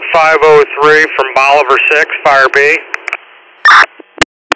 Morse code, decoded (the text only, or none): none